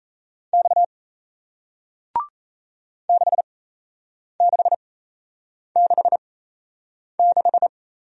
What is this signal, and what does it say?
Morse code, heard as a keyed tone, about 37 wpm, spaced as for 8 wpm, 700 Hz. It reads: X 6666